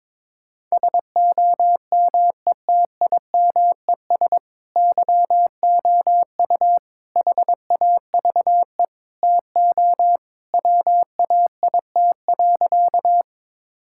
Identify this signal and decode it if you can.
Morse code, heard as a keyed tone, about 22 wpm, 700 Hz. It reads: SOMETIMES YOU HAVE TO WAIT.
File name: practice / 22wpm / Sentences with 3-Word Phrases - 22wpm